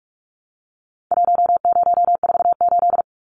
Morse code, heard as a keyed tone, about 44 wpm, 700 Hz. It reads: 1048
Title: Numbers - Contest Style - 4-Digit 45wpm